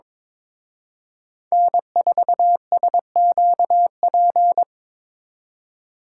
Morse code, heard as a keyed tone, about 25 wpm, 700 Hz. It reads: N4SQP
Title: Speed Racing - USA 1x3 Call Signs 22wpm